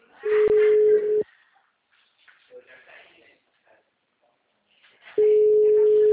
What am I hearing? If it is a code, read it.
Ringback tone (Europe, 425 Hz), with speech.